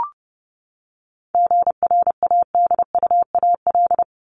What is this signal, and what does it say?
Morse code, heard as a keyed tone, about 30 wpm, 700 Hz. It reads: GRADUAL
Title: LY Suffix - Top 100 Words with Base Words 30wpm